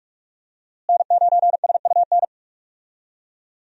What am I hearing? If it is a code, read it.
Morse code, heard as a keyed tone, about 46 wpm, 700 Hz. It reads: N9SUN
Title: USA Call Signs - 5 Characters - Repeated 3x - 45wpm